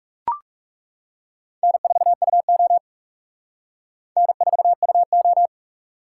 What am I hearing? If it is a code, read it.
Morse code, heard as a keyed tone, about 40 wpm, 700 Hz. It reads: N4UO N4UO